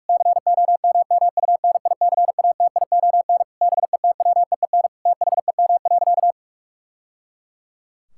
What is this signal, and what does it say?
Morse code, heard as a keyed tone, about 46 wpm, 700 Hz. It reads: COMMUNICATION BETWEEN THEM.
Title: Sentences from Top 1000 Words - Review Part 6 - 45wpm